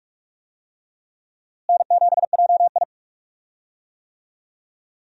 Morse code, heard as a keyed tone, about 46 wpm, 700 Hz. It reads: N7JI
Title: USA Call Signs - 4 Characters - Repeated 3x - 45wpm